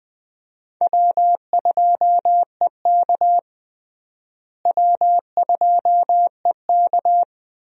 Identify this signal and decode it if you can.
Morse code, heard as a keyed tone, about 20 wpm, 700 Hz. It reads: W2EK W2EK
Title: USA Call Signs - 4 Characters - Repeated 2x - 20wpm